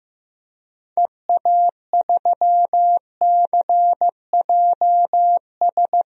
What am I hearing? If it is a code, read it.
Morse code, heard as a keyed tone, about 15 wpm, 700 Hz. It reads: EA3CJS
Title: DXCC-100 Call Signs - 6 Characters - Repeated 2x - 15wpm